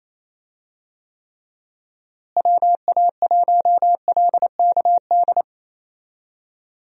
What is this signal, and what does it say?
Morse code, heard as a keyed tone, about 28 wpm, 700 Hz. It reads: WA1LKD
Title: USA Call Signs - 6 Characters - Repeated 3x - 28wpm